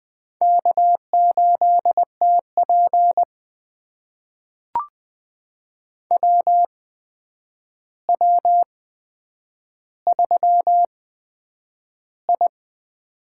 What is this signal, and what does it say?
Morse code, heard as a keyed tone, about 20 wpm, spaced as for 9 wpm, 700 Hz. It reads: K8TP WW3I
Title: USA Call Signs - 4 Characters - 8x Character Spacing - 20wpm